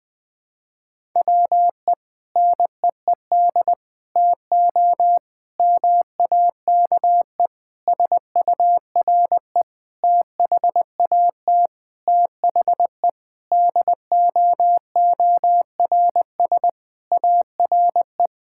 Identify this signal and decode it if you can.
Morse code, heard as a keyed tone, about 20 wpm, 700 Hz. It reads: WE NEED TO MAKE SURE THAT THE DOORS ARE
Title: Sentences with 5-Word Phrases - 20wpm